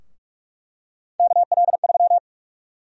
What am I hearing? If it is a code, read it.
Morse code, heard as a keyed tone, about 44 wpm, 700 Hz. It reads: KL3
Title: USA Call Sign Prefixes - Repeated 2x - 45wpm